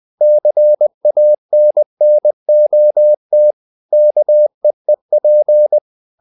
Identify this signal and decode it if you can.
Morse code, heard as a keyed tone, about 20 wpm, 600 Hz. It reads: CANNOT KEEP
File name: the old man and the sea part 2 20wpm